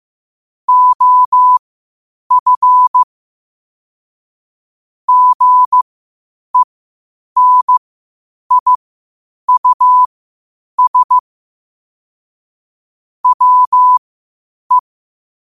Morse code, heard as a keyed tone, about 15 wpm, spaced as for 8 wpm, 1000 Hz. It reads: OF GENIUS WE
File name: qotdfiles / 8wpm